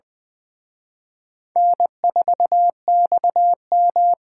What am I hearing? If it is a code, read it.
Morse code, heard as a keyed tone, about 24 wpm, 700 Hz. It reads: N4XM
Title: Speed Racing - USA 1x2 Call Signs 20wpm